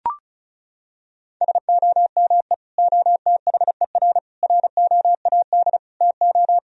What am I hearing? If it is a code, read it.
Morse code, heard as a keyed tone, about 34 wpm, 700 Hz. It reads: SOME OTHER ROAD TO
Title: Sentences from Top 1000 Words - Review Part 4 - 35wpm